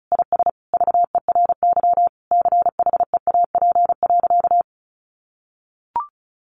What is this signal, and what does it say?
Morse code, heard as a keyed tone, about 34 wpm, 700 Hz. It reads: IS VERY CHEAP.
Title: Sentences from Top 1200 Words 35wpm